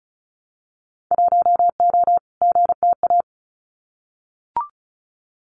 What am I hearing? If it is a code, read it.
Morse code, heard as a keyed tone, about 34 wpm, 700 Hz. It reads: 1O GTA
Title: Winter Field Day Exchanges 35wpm